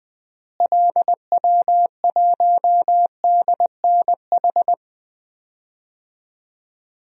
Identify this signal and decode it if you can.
Morse code, heard as a keyed tone, about 20 wpm, 700 Hz. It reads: LW1DNH
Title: DXCC-100 Call Signs - 6 Characters - Repeated 2x - 20wpm